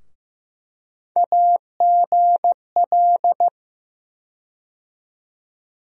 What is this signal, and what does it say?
Morse code, heard as a keyed tone, about 15 wpm, 700 Hz. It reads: AGL
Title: USA Call Sign Postfixes - 3 Letters 15wpm